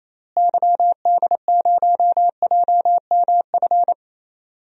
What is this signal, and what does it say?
Morse code, heard as a keyed tone, about 28 wpm, 700 Hz. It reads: YD0JMF
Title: DXCC-100 Call Signs - 6 Characters - Repeated 3x - 28wpm